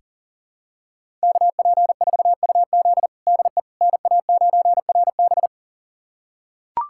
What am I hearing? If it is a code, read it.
Morse code, heard as a keyed tone, about 37 wpm, 700 Hz. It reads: KP4UZ DE NA9RB
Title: USA Callsign DE Callsign - 5 Characters - 40wpm